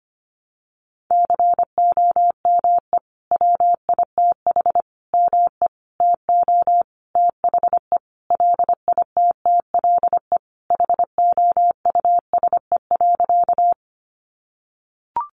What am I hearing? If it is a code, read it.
Morse code, heard as a keyed tone, about 25 wpm, 700 Hz. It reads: COME WITH ME TO THE LITTLE HOUSE.
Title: Sentences from Top 200 Words 25wpm